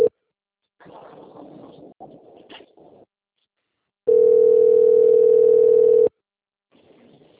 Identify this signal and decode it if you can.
Ringback tone (North America), 440 + 480 Hz.